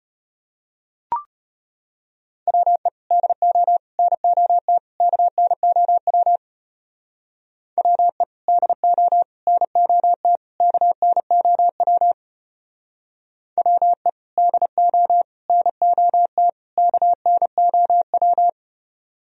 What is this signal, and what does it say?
Morse code, heard as a keyed tone, about 29 wpm, 700 Hz. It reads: WE DO NOT KNOW WE DO NOT KNOW WE DO NOT KNOW
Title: Speed-Racing - Sets of 4 Words 25wpm